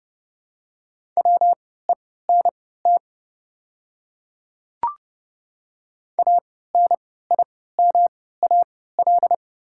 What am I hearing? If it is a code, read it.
Morse code, heard as a keyed tone, about 30 wpm, spaced as for 17 wpm, 700 Hz. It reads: WENT ANIMAL
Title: 3x Character Spacing - Top 200 Words - 30wpm